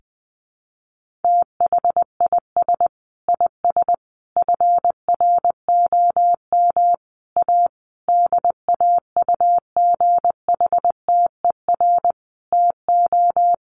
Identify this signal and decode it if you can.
Morse code, heard as a keyed tone, about 20 wpm, 700 Hz. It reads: THIS IS FROM A DAUGHTER TO
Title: Sentences from Top 1500 Words 20wpm